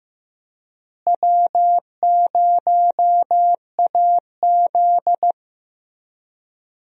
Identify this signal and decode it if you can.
Morse code, heard as a keyed tone, about 15 wpm, 700 Hz. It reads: W0AZ
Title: USA Call Signs - 4 Characters - Repeated 3x - 15wpm